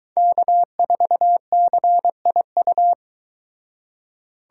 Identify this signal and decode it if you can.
Morse code, heard as a keyed tone, about 23 wpm, 700 Hz. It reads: K4CIU